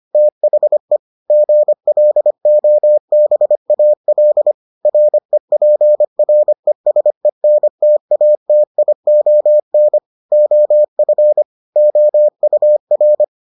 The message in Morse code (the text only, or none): THE GLOBAL REPRESENTATION OF OUR